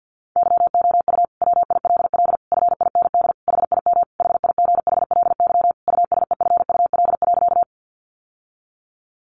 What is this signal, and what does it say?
Morse code, heard as a keyed tone, about 50 wpm, 700 Hz. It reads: YOU WILL FIND HIM HIGHLY USEFUL.
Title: Sentences from Top 1100 Words 50wpm